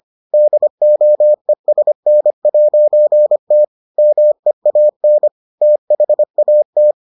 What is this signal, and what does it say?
Morse code, heard as a keyed tone, about 25 wpm, 600 Hz. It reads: DOESN'T MEAN THAT